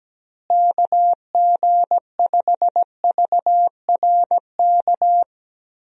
Morse code, heard as a keyed tone, about 17 wpm, 700 Hz. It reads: KG5VRK
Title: USA Call Signs - 6 Characters - Repeated 2x - 17wpm